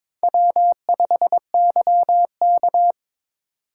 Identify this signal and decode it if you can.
Morse code, heard as a keyed tone, about 22 wpm, 700 Hz. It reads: W5YK
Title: USA Call Signs - 4 Characters - 4x Character Spacing - 22wpm